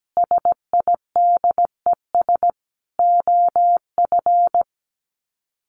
Morse code, heard as a keyed tone, about 17 wpm, 700 Hz. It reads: SIDES OF
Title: Sets of 4 Words 17wpm